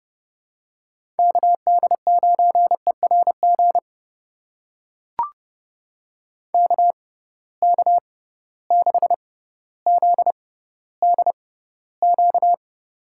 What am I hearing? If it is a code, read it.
Morse code, heard as a keyed tone, about 30 wpm, spaced as for 15 wpm, 700 Hz. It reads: KD9ERG KK6ZDQ
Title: USA Call Signs - 6 Characters - 6x Character Spacing - 30wpm